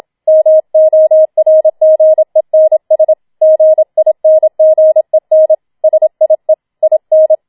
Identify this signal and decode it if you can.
Morse code, heard as a keyed tone, about 27 wpm, 630 Hz. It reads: MORGENS GINGEN SIE IN